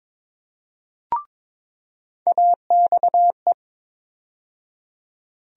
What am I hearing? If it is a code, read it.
Morse code, heard as a keyed tone, about 22 wpm, 700 Hz. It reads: AXE
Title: USA Call Sign Postfixes - 3 Letters 22wpm